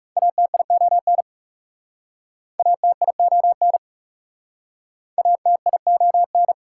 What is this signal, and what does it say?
Morse code, heard as a keyed tone, about 40 wpm, 700 Hz. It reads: ATION ATION ATION